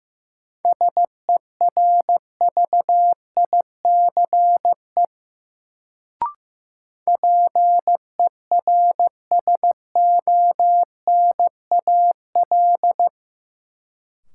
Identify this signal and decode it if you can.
Morse code, heard as a keyed tone, about 15 wpm, 700 Hz. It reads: SERVICE PERSONAL
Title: Sentences from Top 700 Words - New Word Drill - 15wpm